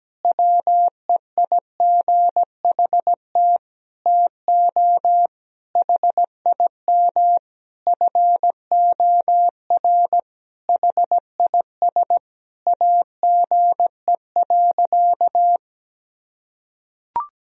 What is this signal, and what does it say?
Morse code, heard as a keyed tone, about 17 wpm, 700 Hz. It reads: WEIGHT TO HIM FOR HIS AGE.
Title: Sentences from Top 1000 Words - Review Part 3 - 17wpm